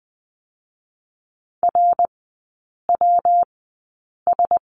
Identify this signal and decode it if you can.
Morse code, heard as a keyed tone, about 20 wpm, spaced as for 8 wpm, 700 Hz. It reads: RWS
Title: 138 Lesson 23e - Letter f - Three Characters 20wpm